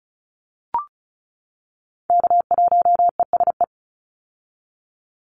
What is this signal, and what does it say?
Morse code, heard as a keyed tone, about 34 wpm, 700 Hz. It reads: K1ESE
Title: Call Signs - 35wpm